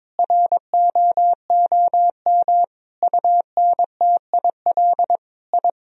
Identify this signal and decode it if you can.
Morse code, heard as a keyed tone, about 22 wpm, 700 Hz. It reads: ROOM UNTIL I
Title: Sentences from Top 1000 Words - Review Part 1 - 22wpm